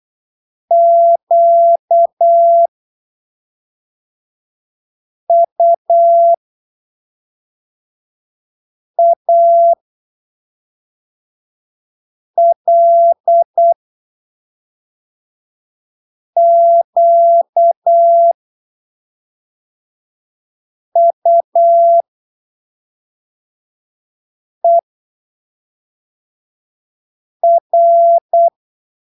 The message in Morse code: QUALQUER